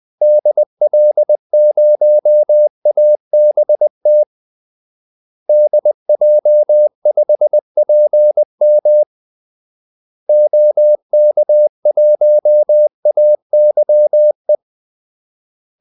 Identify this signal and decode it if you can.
Morse code, heard as a keyed tone, about 20 wpm, 600 Hz. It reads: DL0ABT DJ5PM OK1AYE
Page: Kaldesignaler 20 wpm